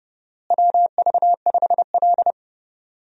Morse code, heard as a keyed tone, about 31 wpm, 700 Hz. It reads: WV5L
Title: USA Call Signs - 4 Characters - Repeated 2x - 30wpm